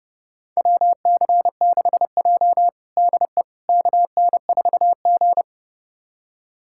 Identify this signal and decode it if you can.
Morse code, heard as a keyed tone, about 30 wpm, 700 Hz. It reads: WC6J DE KN4G
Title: USA Callsign DE Callsign - 4 Characters - 30wpm